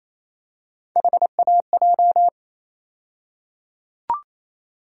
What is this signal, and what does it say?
Morse code, heard as a keyed tone, about 28 wpm, 700 Hz. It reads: HAJ
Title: USA Call Sign Postfixes - 3 Letters 28wpm